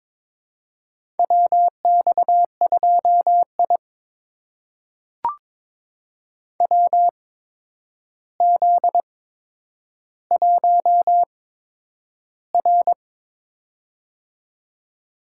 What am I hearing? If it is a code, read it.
Morse code, heard as a keyed tone, about 22 wpm, spaced as for 9 wpm, 700 Hz. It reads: WX2I WZ1R
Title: USA Call Signs - 4 Characters - 8x Character Spacing - 22wpm